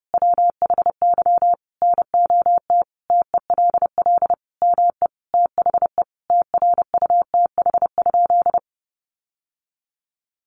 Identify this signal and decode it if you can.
Morse code, heard as a keyed tone, about 28 wpm, 700 Hz. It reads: WHY NOT TELL ME THE TRUTH?
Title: Sentences from Top 800 Words 30wpm